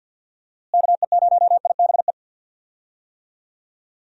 Morse code, heard as a keyed tone, about 46 wpm, 700 Hz. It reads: KE0IBE
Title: USA Call Signs - 6 Characters - Repeated 3x - 50wpm